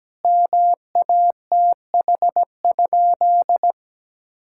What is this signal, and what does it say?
Morse code, heard as a keyed tone, about 17 wpm, 700 Hz. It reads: MATH?
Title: Sentences from Top 1000 Words - Review Part 6 - 17wpm